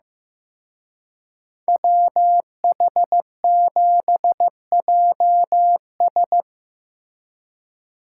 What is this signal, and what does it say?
Morse code, heard as a keyed tone, about 18 wpm, 700 Hz. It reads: WH7JS
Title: Speed Racing - USA 2x2 Call Signs 15wpm